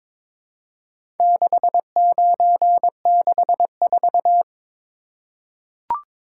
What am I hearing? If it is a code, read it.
Morse code, heard as a keyed tone, about 23 wpm, 700 Hz. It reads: 6964
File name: practice / 22wpm / Four Letter-Number Combinations - 22wpm